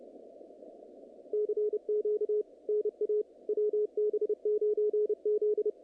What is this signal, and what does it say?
Morse code, heard as a keyed tone, about 29 wpm, 440 Hz, with radio noise. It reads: CQ NA WB9Z
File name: contests / naqp-cw-k5tr-op-k5pi- -2200